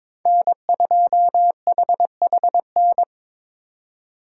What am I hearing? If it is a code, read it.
Morse code, heard as a keyed tone, about 26 wpm, 700 Hz. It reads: N2HHN
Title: Speed Racing - USA 1x3 Call Signs 22wpm